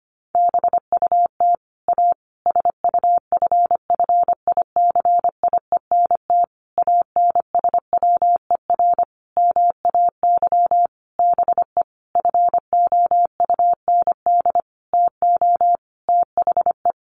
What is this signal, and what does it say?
Morse code, heard as a keyed tone, about 25 wpm, 700 Hz. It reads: BUT A SUFFICIENT ANSWER MAY BE FOUND TO THE
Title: Sentences from Top 1400 Words 25wpm